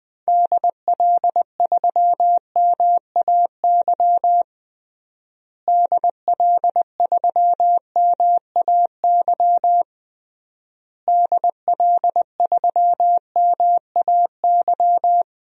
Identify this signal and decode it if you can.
Morse code, heard as a keyed tone, about 20 wpm, 700 Hz. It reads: DL3MAY DL3MAY DL3MAY
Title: DXCC-100 Call Signs - 6 Characters - Repeated 3x - 20wpm